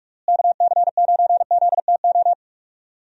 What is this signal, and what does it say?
Morse code, heard as a keyed tone, about 41 wpm, 700 Hz. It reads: KC9ZTO
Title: USA Call Signs - 6 Characters - Repeated 3x - 45wpm